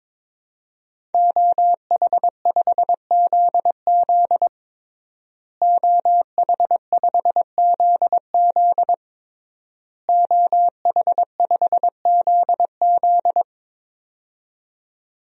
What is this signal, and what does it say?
Morse code, heard as a keyed tone, about 22 wpm, 700 Hz. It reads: OH5ZZ OH5ZZ OH5ZZ
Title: DXCC-100 Call Signs - 5 Characters - Repeated 3x - 22wpm